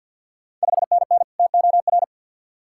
Morse code, heard as a keyed tone, about 51 wpm, 690 Hz. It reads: 5NN TOR
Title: Ontario QSO Party Exchange 50wpm